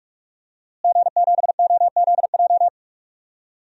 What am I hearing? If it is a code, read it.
Morse code, heard as a keyed tone, about 46 wpm, 700 Hz. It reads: G7OZJ
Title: DXCC-100 Call Signs - 5 Characters - Repeated 3x - 45wpm